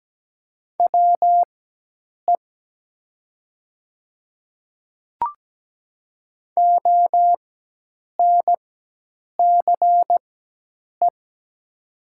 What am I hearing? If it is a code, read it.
Morse code, heard as a keyed tone, about 17 wpm, spaced as for 8 wpm, 700 Hz. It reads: WE ONCE
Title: 4x Character Spacing - Top 300 Words - 17wpm